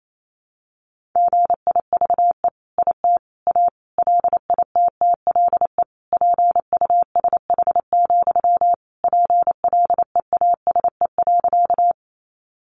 Morse code, heard as a keyed tone, about 28 wpm, 700 Hz. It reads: GIVE IT A LITTLE PUSH, PLEASE.
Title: Sentences from Top 1500 Words 28wpm